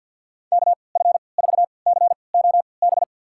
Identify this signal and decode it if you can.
Morse code, heard as a keyed tone, about 52 wpm, spaced as for 29 wpm, 700 Hz. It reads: KF4COB
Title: USA Call Signs - 6 Characters - 3x Character Spacing - 50wpm